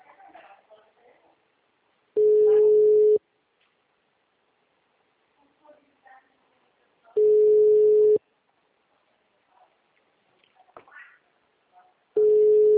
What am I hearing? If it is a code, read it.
Ringback tone (Europe, 425 Hz), with speech.